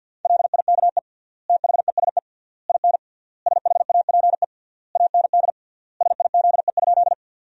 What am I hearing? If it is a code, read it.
Morse code, heard as a keyed tone, about 46 wpm, 700 Hz. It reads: LIKE THESE IN SHAPE AND SIZE?
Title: 3x Word-Spacing - Sentences from Top 500 Words - 50wpm